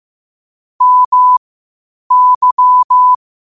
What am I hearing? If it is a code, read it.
Morse code, heard as a keyed tone, about 15 wpm, spaced as for 8 wpm, 1000 Hz. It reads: MY
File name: qotdfiles / 8wpm